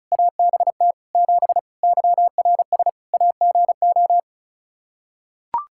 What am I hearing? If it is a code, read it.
Morse code, heard as a keyed tone, about 34 wpm, 700 Hz. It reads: ABT 7 YRS AGO
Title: Practice QSO - Exchange 3 - More Info - 35wpm